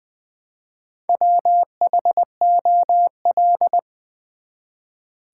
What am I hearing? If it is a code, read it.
Morse code, heard as a keyed tone, about 20 wpm, 700 Hz. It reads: WHOL
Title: Build A Word - 1 Letter At A Time - 5-Letter Words - 20wpm